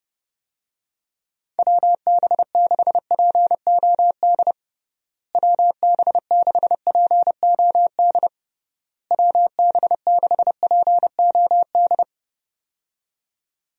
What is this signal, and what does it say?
Morse code, heard as a keyed tone, about 32 wpm, 700 Hz. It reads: WB6POD WB6POD WB6POD
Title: USA Call Signs - 6 Characters - Repeated 3x - 30wpm